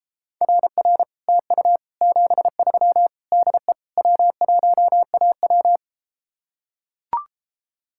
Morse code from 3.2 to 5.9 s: DE W1AW

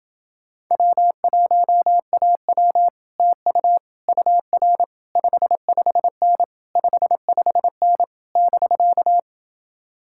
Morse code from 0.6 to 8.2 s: W1AW TU UR 55N 55N

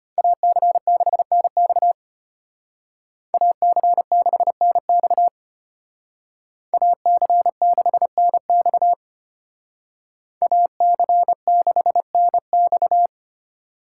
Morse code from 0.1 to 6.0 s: AC6NX AC6NX

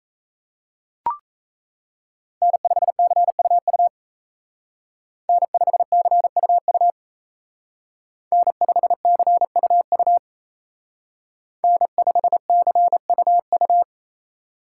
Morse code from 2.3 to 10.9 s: N5CUU N5CUU N5CUU

